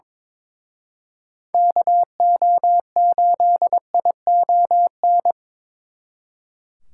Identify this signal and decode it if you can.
Morse code, heard as a keyed tone, about 27 wpm, 700 Hz. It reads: KO8ION
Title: Speed Racing - USA 2x3 Call Signs 22wpm